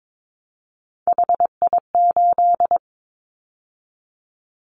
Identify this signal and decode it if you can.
Morse code, heard as a keyed tone, about 22 wpm, 700 Hz. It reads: HI8
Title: Callsign Prefixes for Top 100 DXCC - 22wpm